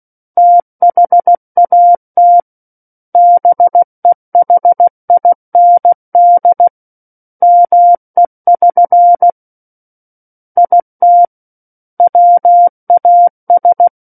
Morse code, heard as a keyed tone, about 16 wpm, 700 Hz. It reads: THAT BEHIND ME<SN> IT WAS
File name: Book One / 07-Chapter 7 - 16 WPM